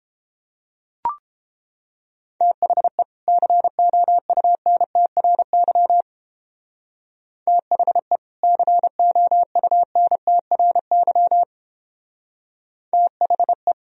Morse code, heard as a keyed tone, about 31 wpm, 700 Hz. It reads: THE COUNTRY THE COUNTRY THE